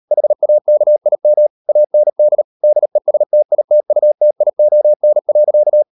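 Morse code, heard as a keyed tone, about 38 wpm, 600 Hz. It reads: HAKIM AND DESTITUTION.
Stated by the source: Morse code practice at 38 WPM, sending Ken Follett's novel Never in English, from CW med Gnister.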